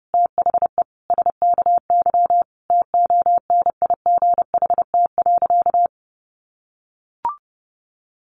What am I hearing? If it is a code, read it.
Morse code, heard as a keyed tone, about 30 wpm, 700 Hz. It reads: THE SKY TONIGHT.
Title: Sentences from Top 1300 Words 30wpm